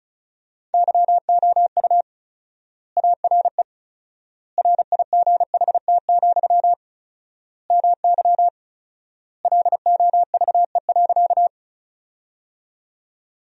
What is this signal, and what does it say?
Morse code, heard as a keyed tone, about 36 wpm, 700 Hz. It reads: YOU ARE RIGHT, MY LOVE.
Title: 4x Word-Spacing - Sentences from Top 400 Words - 35wpm